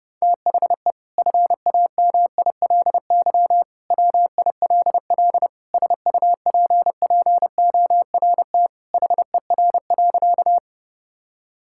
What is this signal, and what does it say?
Morse code, heard as a keyed tone, about 30 wpm, 700 Hz. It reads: THE FAMILY WILL SUPPORT HER.
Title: Sentences from Top 1000 Words - Review Part 5 - 30wpm